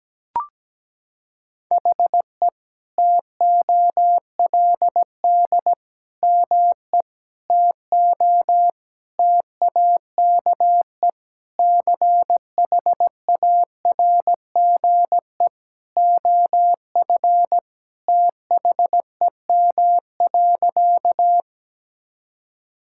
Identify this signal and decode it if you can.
Morse code, heard as a keyed tone, about 17 wpm, 700 Hz. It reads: HE TOLD ME TO TAKE CHARGE OF THEM.
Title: Sentences from Top 1000 Words - Review Part 6 - 17wpm